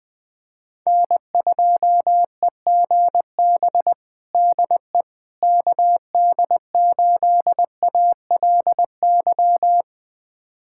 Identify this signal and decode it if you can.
Morse code, heard as a keyed tone, about 20 wpm, 700 Hz. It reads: N2EGB DE KD8ALY
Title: USA Callsign DE Callsign 20wpm